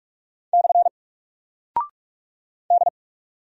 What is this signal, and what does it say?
Morse code, heard as a keyed tone, about 44 wpm, 700 Hz. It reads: / D
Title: Single Character (Letter:Number:Punctuation:Prosign) 45wpm